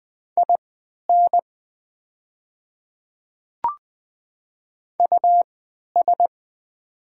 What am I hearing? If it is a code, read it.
Morse code, heard as a keyed tone, about 20 wpm, spaced as for 11 wpm, 690 Hz. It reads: IN US
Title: 3x Character Spacing - Top 300 Words - 20wpm